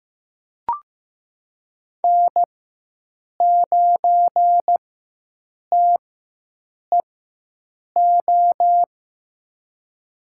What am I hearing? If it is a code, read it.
Morse code, heard as a keyed tone, about 15 wpm, spaced as for 7 wpm, 700 Hz. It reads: N9TEO
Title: USA Call Signs - 4x Character Spacing - 15wpm